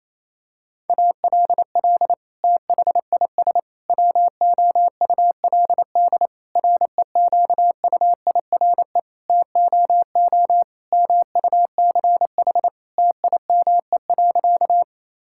Morse code, heard as a keyed tone, about 27 wpm, 700 Hz. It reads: ALL THIS WOULD REQUIRE TOO MUCH TIME.
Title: Sentences from Top 1000 Words - Review Part 6 - 28wpm